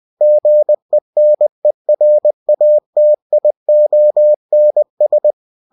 Morse code, heard as a keyed tone, about 20 wpm, 600 Hz. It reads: GENERATIONS
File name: cw-books / winnie-the-pooh / winnie-20wpm-page-0120